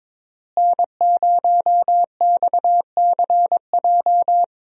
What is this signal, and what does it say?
Morse code, heard as a keyed tone, about 22 wpm, 700 Hz. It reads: N0XCJ
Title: USA Call Signs - 5 Characters - Repeated 3x - 22wpm